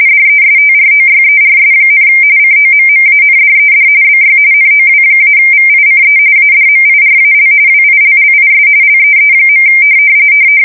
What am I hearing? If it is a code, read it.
RTTY, about 45.45 baud. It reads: DESTROYERS, COMMANDED BY NOW-VICE ADMIRAL WILLIS LEE.
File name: fsk170 / USS-Massachusetts